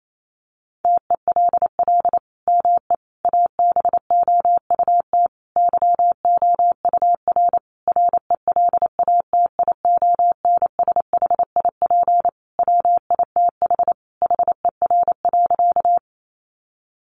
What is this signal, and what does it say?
Morse code, heard as a keyed tone, about 28 wpm, 700 Hz. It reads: TELL ME ABOUT YOUR RELATIONSHIP WITH HER.
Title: Sentences from Top 1100 Words 28wpm